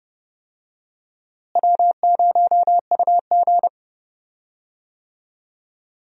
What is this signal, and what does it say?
Morse code, heard as a keyed tone, about 30 wpm, 700 Hz. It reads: W0UG
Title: USA Call Signs - 4 Characters - Repeated 3x - 30wpm